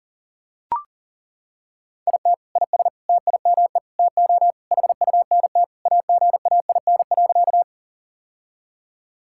Morse code, heard as a keyed tone, about 37 wpm, 700 Hz. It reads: IT IS TIME TO HUNT AGAIN.
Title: Sentences from Top 1600 Words 40wpm